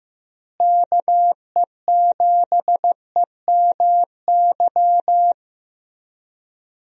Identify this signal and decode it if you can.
Morse code, heard as a keyed tone, about 15 wpm, 700 Hz. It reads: KE7EMY
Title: USA Call Signs - 6 Characters - Repeated 3x - 15wpm